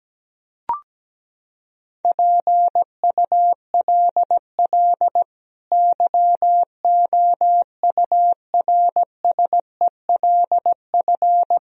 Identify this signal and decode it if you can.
Morse code, heard as a keyed tone, about 17 wpm, 700 Hz. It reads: PULL YOURSELF
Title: Sentences from Top 1000 Words - Review Part 2 - 17wpm